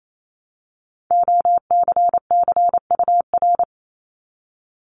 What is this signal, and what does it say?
Morse code, heard as a keyed tone, about 28 wpm, 700 Hz. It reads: OCCUR
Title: Multi-letter Phonemes 28wpm